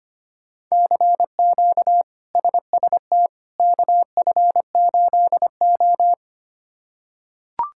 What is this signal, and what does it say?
Morse code, heard as a keyed tone, about 25 wpm, 700 Hz. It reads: CQ SST KF8O
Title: K1USN SST QSO Practice - 25wpm